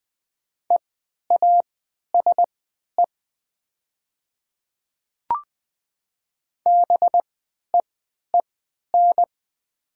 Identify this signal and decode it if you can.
Morse code, heard as a keyed tone, about 20 wpm, spaced as for 11 wpm, 700 Hz. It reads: EASE BEEN
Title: 3x Character Spacing - Top 500 Words - 20wpm